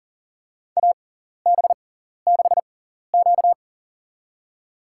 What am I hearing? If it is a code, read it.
Morse code, heard as a keyed tone, about 44 wpm, spaced as for 14 wpm, 700 Hz. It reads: AB6Q
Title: USA Call Signs - 4 Characters - 6x Character Spacing - 40wpm